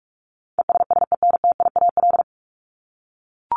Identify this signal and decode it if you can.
Morse code, heard as a keyed tone, about 44 wpm, 700 Hz. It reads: ESSENTIAL
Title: LY Suffix - Top 100 Words with Base Words 45wpm